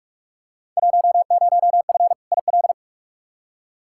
Morse code, heard as a keyed tone, about 46 wpm, 700 Hz. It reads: 10F IL
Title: Field Day Exchanges Encore - 45wpm